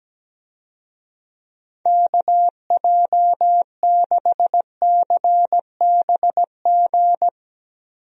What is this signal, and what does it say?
Morse code, heard as a keyed tone, about 17 wpm, 700 Hz. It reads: KJ6CBG
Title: USA Call Signs - 6 Characters - 17wpm